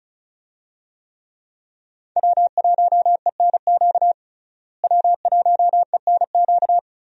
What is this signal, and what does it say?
Morse code, heard as a keyed tone, about 36 wpm, 700 Hz. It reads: W1ENQ W1ENQ
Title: USA Call Signs - 5 Characters - Repeated 3x - 35wpm